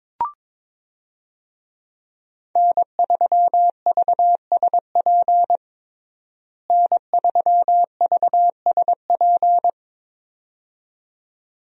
Morse code, heard as a keyed tone, about 22 wpm, 700 Hz. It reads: N3VSP N3VSP
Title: USA Call Signs - 5 Characters - Repeated 2x - 22wpm